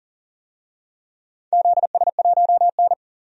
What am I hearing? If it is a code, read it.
Morse code, heard as a keyed tone, about 40 wpm, 700 Hz. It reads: ZS1N
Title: DXCC-100 Call Signs - 4 Characters - 40wpm